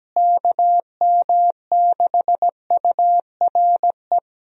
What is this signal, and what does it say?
Morse code, heard as a keyed tone, about 17 wpm, 700 Hz. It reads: KM6URE
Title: USA Call Signs - 6 Characters - Repeated 3x - 17wpm